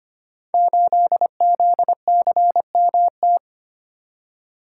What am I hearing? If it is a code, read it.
Morse code, heard as a keyed tone, about 25 wpm, 700 Hz. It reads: 8ZCMT
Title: Code Groups - 5 Letter:Numbers - 25wpm